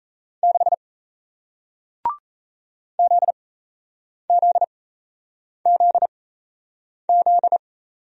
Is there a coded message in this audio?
Yes: Morse code from 0.4 to 7.6 s, heard as a keyed tone, about 38 wpm, spaced as for 8 wpm, 700 Hz. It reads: 6 ZZZZ